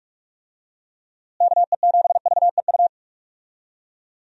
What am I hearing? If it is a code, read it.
Morse code, heard as a keyed tone, about 41 wpm, 700 Hz. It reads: KE7VEU
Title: USA Call Signs - 6 Characters - Repeated 2x - 45wpm